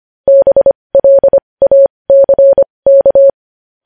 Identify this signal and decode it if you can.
Morse code, heard as a keyed tone, about 25 wpm, spaced as for 20 wpm, 550 Hz. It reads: BLACK
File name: podcast / News headlines-1x-25x20